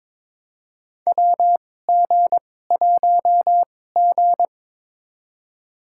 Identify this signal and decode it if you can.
Morse code, heard as a keyed tone, about 21 wpm, spaced as for 16 wpm, 700 Hz. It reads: WG1G
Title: USA Call Signs - 4 Characters - 2x Character Spacing - 22wpm